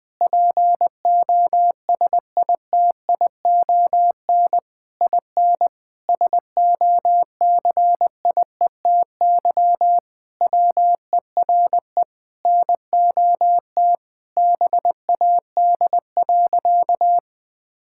Morse code, heard as a keyed tone, about 20 wpm, 700 Hz. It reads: POSITION IN SOCIETY WERE NOT BAD.
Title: Sentences from Top 1000 Words - Review Part 5 - 20wpm